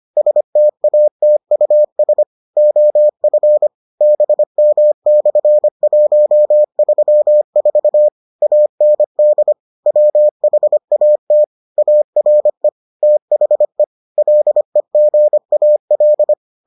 Morse code, heard as a keyed tone, about 25 wpm, 600 Hz. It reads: STATUS OF BM/134 AND WHAT ARE THE LEGAL